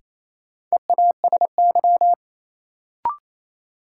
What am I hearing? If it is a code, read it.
Morse code, heard as a keyed tone, about 28 wpm, 700 Hz. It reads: EASY
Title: Sentences from Top 700 Words - New Word Drill - 28wpm